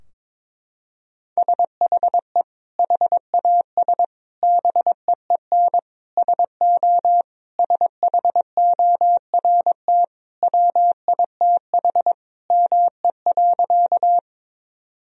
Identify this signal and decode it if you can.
Morse code, heard as a keyed tone, about 22 wpm, 700 Hz. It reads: SHE HAS BEEN SO SHORT WITH ME.
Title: Sentences from Top 1000 Words - Review Part 2 - 22wpm